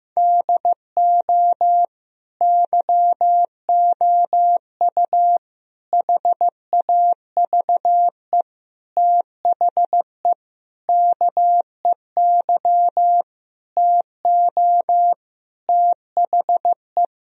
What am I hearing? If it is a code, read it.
Morse code, heard as a keyed tone, about 15 wpm, 700 Hz. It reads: DO YOU HAVE THE KEY TO THE
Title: Sentences from Top 1000 Words - Review Part 4 - 15wpm